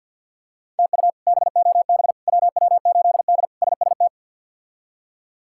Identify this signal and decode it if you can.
Morse code, heard as a keyed tone, about 52 wpm, 700 Hz. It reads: TU BOB WW8D SST
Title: K1USN SST QSO Practice - 50wpm